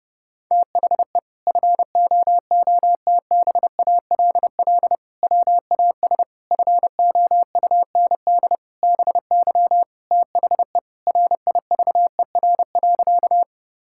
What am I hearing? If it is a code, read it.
Morse code, heard as a keyed tone, about 30 wpm, 700 Hz. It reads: THE FOOTBALL WAS FOUND BY THE RIVER.
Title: Sentences from Top 1600 Words 30wpm